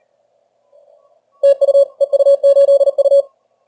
Morse code, heard as a keyed tone, about 37 wpm, 560 Hz. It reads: TU EU8U
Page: Сегодня решил проверить динамический диапазон приемника в экстримальных условиях теста. Плотность станций на диапазонах большая.